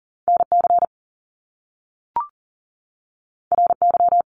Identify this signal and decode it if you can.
Morse code, heard as a keyed tone, about 38 wpm, 700 Hz. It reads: NC RY
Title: Top Two Letter Combinations in Words 40wpm